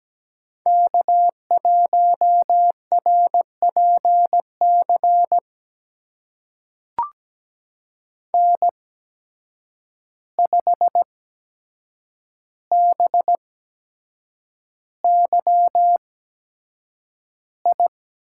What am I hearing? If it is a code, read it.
Morse code, heard as a keyed tone, about 17 wpm, spaced as for 7 wpm, 700 Hz. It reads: K1RPC N5BYI
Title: USA Call Signs - 5 Characters - 8x Character Spacing - 17wpm